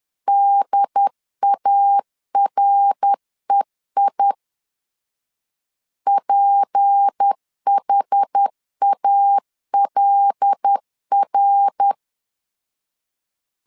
Morse code, heard as a keyed tone, about 10 wpm, 800 Hz. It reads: DAREI PHALR
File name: LEZIONI / 6 PKX0 / 6 P 10wpm 2432